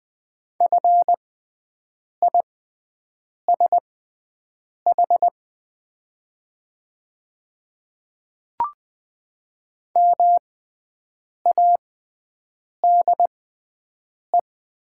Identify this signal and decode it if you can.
Morse code, heard as a keyed tone, about 20 wpm, spaced as for 7 wpm, 700 Hz. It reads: FISH MADE
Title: 6x Character Spacing - Top 500 Words - 20wpm